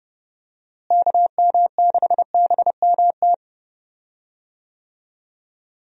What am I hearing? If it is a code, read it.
Morse code, heard as a keyed tone, about 30 wpm, 700 Hz. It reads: KM6BMT
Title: USA Call Signs - 6 Characters - Repeated 3x - 30wpm